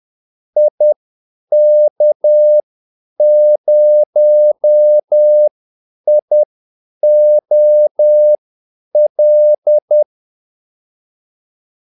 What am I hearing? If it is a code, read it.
Morse code, heard as a keyed tone, about 10 wpm, spaced as for 8 wpm, 600 Hz. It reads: IK0IOL
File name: Callsigns-10-8wpm